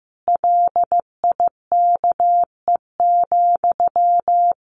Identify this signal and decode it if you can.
Morse code, heard as a keyed tone, about 14 wpm, 700 Hz. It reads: LIKE,
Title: Sentences from Top 1400 Words 15wpm